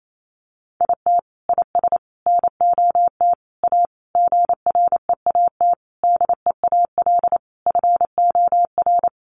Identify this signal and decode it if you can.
Morse code, heard as a keyed tone, about 28 wpm, 700 Hz. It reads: IT IS NOT A GREAT DEAL FOR
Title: Sentences from Top 800 Words 28wpm